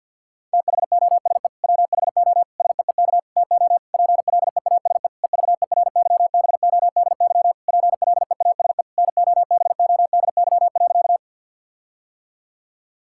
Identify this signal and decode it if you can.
Morse code, heard as a keyed tone, about 50 wpm, 700 Hz. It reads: THOSE WHO SEEK TO PLEASE EVERYBODY PLEASE NOBODY.
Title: Sentences from Top 1600 Words 50wpm